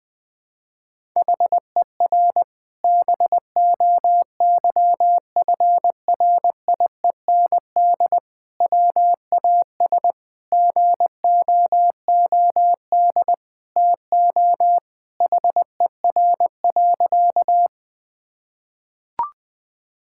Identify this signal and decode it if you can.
Morse code, heard as a keyed tone, about 21 wpm, 700 Hz. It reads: HER BOYFRIEND WAS GOOD TO HER.
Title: Sentences from Top 1000 Words - Review Part 5 - 20wpm